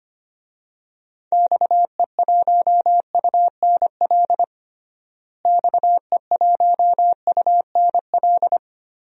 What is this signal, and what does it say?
Morse code, heard as a keyed tone, about 25 wpm, 700 Hz. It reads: XE1UNL XE1UNL
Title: DXCC-100 Call Signs - 6 Characters - Repeated 3x - 25wpm